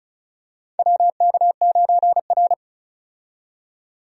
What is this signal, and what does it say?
Morse code, heard as a keyed tone, about 34 wpm, 700 Hz. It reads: WK9R
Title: USA Call Signs - 4 Characters - 35wpm